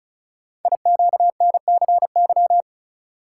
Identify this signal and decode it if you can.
Morse code, heard as a keyed tone, about 36 wpm, 700 Hz. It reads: IQNCY
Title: Code Groups - 5 Letters - 35wpm